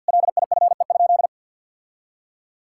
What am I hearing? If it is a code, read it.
Morse code, heard as a keyed tone, about 48 wpm, 700 Hz. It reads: LIFE?
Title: Sentences from Top 1000 Words - Review Part 5 - 50wpm